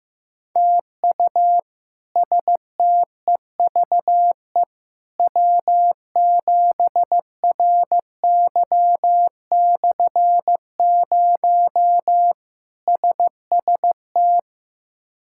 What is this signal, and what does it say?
Morse code, heard as a keyed tone, about 15 wpm, 700 Hz. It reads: TU STEVE W7RY/0 SST
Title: K1USN SST QSO Practice - 15wpm